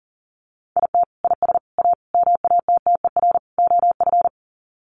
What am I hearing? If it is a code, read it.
Morse code, heard as a keyed tone, about 40 wpm, 700 Hz. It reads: IT IS A MATTER OF
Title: Sets of 5 Words Encore2 40wpm